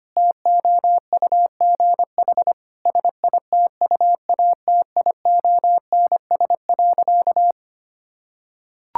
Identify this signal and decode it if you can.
Morse code, heard as a keyed tone, about 25 wpm, 700 Hz. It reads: TOUGH SITUATIONS.
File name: practice / 25wpm / Difficult Sentences with Hints - 25wpm